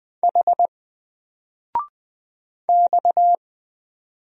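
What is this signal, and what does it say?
Morse code, heard as a keyed tone, about 20 wpm, 700 Hz. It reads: H X
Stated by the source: Single letter and number Morse drill at 20 wpm, from N4LCD.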